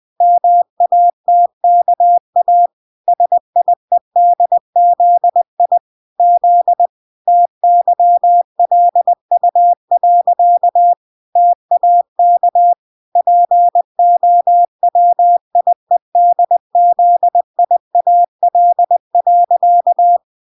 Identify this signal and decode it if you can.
Morse code, heard as a keyed tone, about 20 wpm, 700 Hz. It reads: MATKA SIEDZI Z TYLU. TAK POWIEDZIAL.